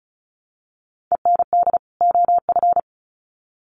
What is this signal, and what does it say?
Morse code, heard as a keyed tone, about 34 wpm, 700 Hz. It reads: END OF
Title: Sets of 2 Words 35wpm